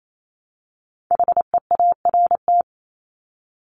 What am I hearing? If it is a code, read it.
Morse code, heard as a keyed tone, about 28 wpm, 700 Hz. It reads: HEART
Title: Top 1001-1250 Words 28wpm